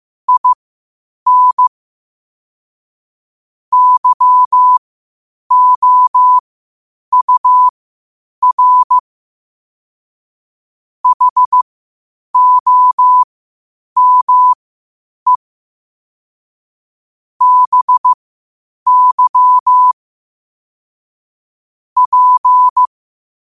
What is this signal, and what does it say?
Morse code, heard as a keyed tone, about 15 wpm, spaced as for 8 wpm, 1000 Hz. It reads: IN YOUR HOME BY P
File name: qotdfiles / 8wpm